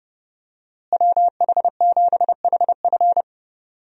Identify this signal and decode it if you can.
Morse code, heard as a keyed tone, about 32 wpm, 700 Hz. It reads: WH7HF
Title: USA Call Signs - 5 Characters - Repeated 3x - 30wpm